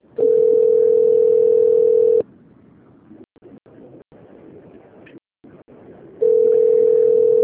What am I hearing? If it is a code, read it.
Ringback tone (North America), 440 + 480 Hz.
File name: rg-1600-unknown- - - .7807